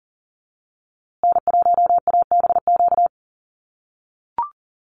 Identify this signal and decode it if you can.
Morse code, heard as a keyed tone, about 40 wpm, 700 Hz. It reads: N1ABQ
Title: USA Call Signs Repeated 3x 40wpm